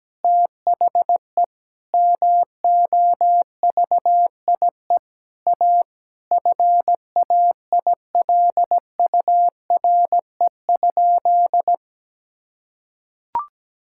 Morse code, heard as a keyed tone, about 17 wpm, 700 Hz. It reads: THE MOVIE A FAILURE?
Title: Sentences from Top 1000 Words - Review Part 6 - 17wpm